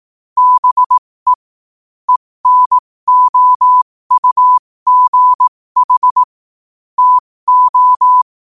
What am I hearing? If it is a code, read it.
Morse code, heard as a keyed tone, about 18 wpm, spaced as for 15 wpm, 1000 Hz. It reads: BE ENOUGH TO
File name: qotdfiles / 15wpm